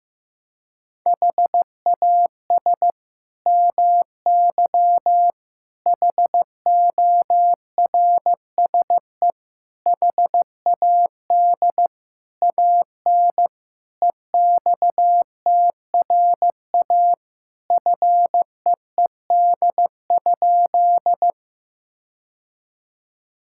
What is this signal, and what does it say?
Morse code, heard as a keyed tone, about 15 wpm, 700 Hz. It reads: HAS MY HORSE HAD AN EXTRA FEED?
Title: Sentences from Top 1600 Words 15wpm